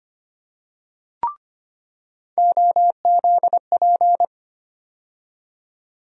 Morse code, heard as a keyed tone, about 25 wpm, 700 Hz. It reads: OZP
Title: USA Call Sign Postfixes - 3 Letters 25wpm